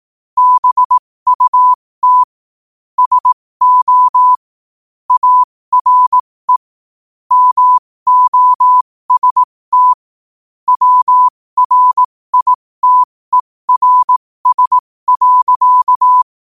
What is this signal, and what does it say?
Morse code, heard as a keyed tone, about 18 wpm, spaced as for 15 wpm, 1000 Hz. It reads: BUT SO ARE MOST WRITERS.
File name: qotdfiles / 15wpm